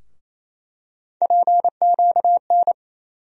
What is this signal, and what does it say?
Morse code, heard as a keyed tone, about 28 wpm, 700 Hz. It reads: PQN
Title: USA Call Sign Postfixes - 3 Letters 28wpm